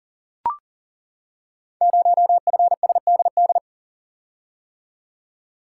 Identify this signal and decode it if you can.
Morse code, heard as a keyed tone, about 38 wpm, 690 Hz. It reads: 0FSDD
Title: Code Groups - 5 Letter:Numbers - 40wpm